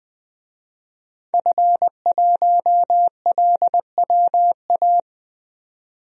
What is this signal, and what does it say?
Morse code, heard as a keyed tone, about 21 wpm, 700 Hz. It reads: F1LWA
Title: DXCC-100 Call Signs - 5 Characters - 20wpm